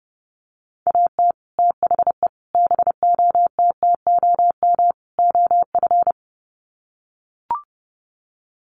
Morse code, heard as a keyed tone, about 30 wpm, 700 Hz. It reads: AT THE BOTTOM OF
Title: Sets of 4 Words 30wpm